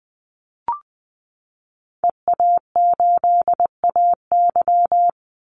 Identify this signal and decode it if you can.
Morse code, heard as a keyed tone, about 20 wpm, 700 Hz. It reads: EA8AY
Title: Call Signs Repeated 3x - 20wpm